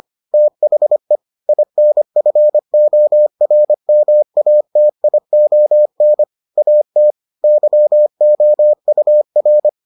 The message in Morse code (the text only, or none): THE INFORMATION AT YOUR